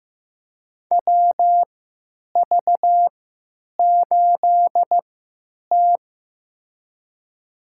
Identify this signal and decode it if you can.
Morse code, heard as a keyed tone, about 15 wpm, spaced as for 9 wpm, 700 Hz. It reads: WV8T
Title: USA Call Signs - 3x Character Spacing - 15wpm